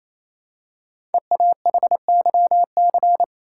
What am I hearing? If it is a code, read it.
Morse code, heard as a keyed tone, about 27 wpm, 700 Hz. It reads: EAHYC
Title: Code Groups - 5 Letters - 28wpm